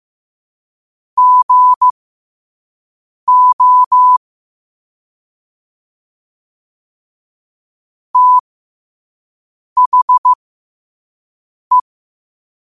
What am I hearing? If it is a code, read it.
Morse code, heard as a keyed tone, about 15 wpm, spaced as for 5 wpm, 1000 Hz. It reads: GO THE
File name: 5wpm